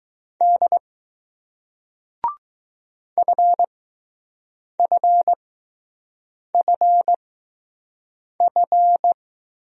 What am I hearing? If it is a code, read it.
Morse code, heard as a keyed tone, about 20 wpm, spaced as for 6 wpm, 700 Hz. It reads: D FFFF